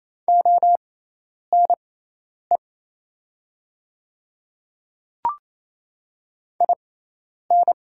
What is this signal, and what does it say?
Morse code, heard as a keyed tone, about 27 wpm, spaced as for 10 wpm, 700 Hz. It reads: ONE IN
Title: 6x Character Spacing - Top 100 Words - 28wpm